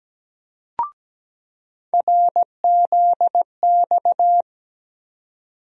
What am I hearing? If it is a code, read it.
Morse code, heard as a keyed tone, about 17 wpm, 700 Hz. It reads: RZX
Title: USA Call Sign Postfixes - 3 Letters 17wpm